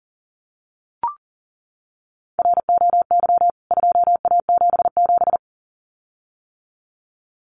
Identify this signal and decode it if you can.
Morse code, heard as a keyed tone, about 40 wpm, 700 Hz. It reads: ROY 2A77
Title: CWT Exchange 40wpm